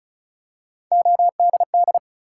Morse code, heard as a keyed tone, about 36 wpm, 700 Hz. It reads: ODD
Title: Three Letter Words - 35wpm